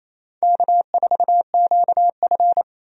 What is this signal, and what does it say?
Morse code, heard as a keyed tone, about 28 wpm, 700 Hz. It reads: K4QF
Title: USA Call Signs - 4 Characters - 28wpm